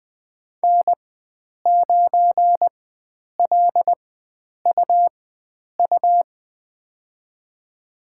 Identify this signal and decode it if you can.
Morse code, heard as a keyed tone, about 20 wpm, spaced as for 9 wpm, 700 Hz. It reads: N9LUU
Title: USA Call Signs - 5 Characters - 4x Character Spacing - 20wpm